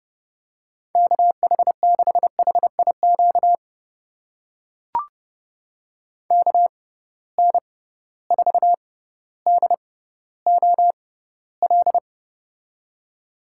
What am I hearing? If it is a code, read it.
Morse code, heard as a keyed tone, about 30 wpm, spaced as for 15 wpm, 700 Hz. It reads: KH6HIQ KN4DOL
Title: USA Call Signs - 6 Characters - 6x Character Spacing - 30wpm